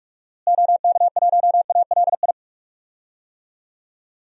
Morse code, heard as a keyed tone, about 41 wpm, 700 Hz. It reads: OK1ALI
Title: DXCC-100 Call Signs - 6 Characters - Repeated 3x - 45wpm